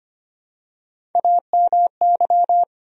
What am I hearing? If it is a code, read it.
Morse code, heard as a keyed tone, about 25 wpm, 700 Hz. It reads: AMY
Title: USA Common Names - Max 4 Letters - 25wpm